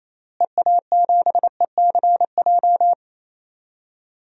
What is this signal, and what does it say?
Morse code, heard as a keyed tone, about 28 wpm, 700 Hz. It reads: EA7ECJ
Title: DXCC-100 Call Signs - 6 Characters - Repeated 3x - 28wpm